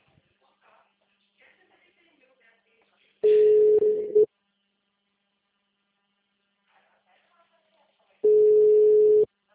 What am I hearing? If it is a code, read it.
Ringback tone (Europe, 425 Hz), with speech.
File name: out- -2021- - - .539